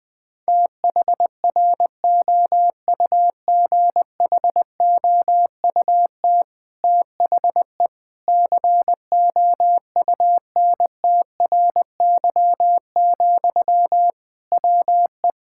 Morse code, heard as a keyed tone, about 20 wpm, 700 Hz. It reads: THROUGHOUT THE COUNTRY, WE
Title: Sentences from Top 1000 Words - Review Part 5 - 20wpm